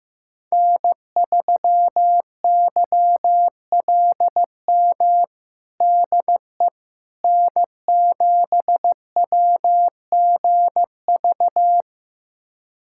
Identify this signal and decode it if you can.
Morse code, heard as a keyed tone, about 15 wpm, 700 Hz. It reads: N3YLM DE N7WGV
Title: USA Callsign DE Callsign - 5 Characters - 15wpm